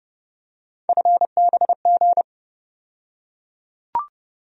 Morse code, heard as a keyed tone, about 30 wpm, 700 Hz. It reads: FBG
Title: USA Call Sign Postfixes - 3 Letters 30wpm